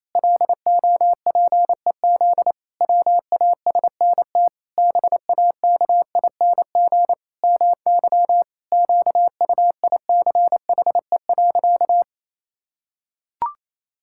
Morse code, heard as a keyed tone, about 28 wpm, 700 Hz. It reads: LOPEZ WASNT BAKING MY QUICHE.
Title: Pangram Challenge v2 - 28wpm